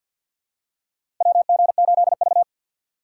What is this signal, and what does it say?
Morse code, heard as a keyed tone, about 52 wpm, 700 Hz. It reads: WG8V
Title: USA Call Signs - 4 Characters - 6x Character Spacing - 50wpm